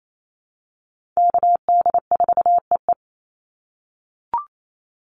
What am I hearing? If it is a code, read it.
Morse code, heard as a keyed tone, about 28 wpm, 700 Hz. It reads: KD4EE
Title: Call Signs - 28wpm